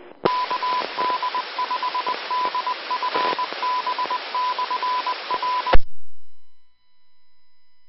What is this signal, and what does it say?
Morse code, heard as a keyed tone, about 20 wpm, 1000 Hz. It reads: KD5DFB/R
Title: Austin, TX Amateur Repeaters